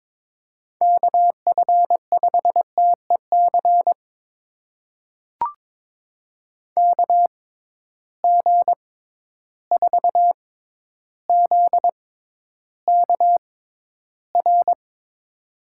Morse code, heard as a keyed tone, about 22 wpm, spaced as for 11 wpm, 700 Hz. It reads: KF5TEC KG4ZKR
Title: USA Call Signs - 6 Characters - 6x Character Spacing - 22wpm